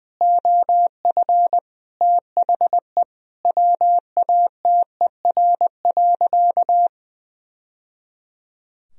Morse code, heard as a keyed tone, about 20 wpm, 700 Hz. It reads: OF THE WATER.
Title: Sentences from Top 1000 Words - Review Part 3 - 20wpm